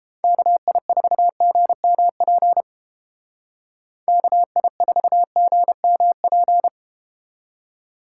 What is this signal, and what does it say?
Morse code, heard as a keyed tone, about 32 wpm, 700 Hz. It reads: KI4GMP KI4GMP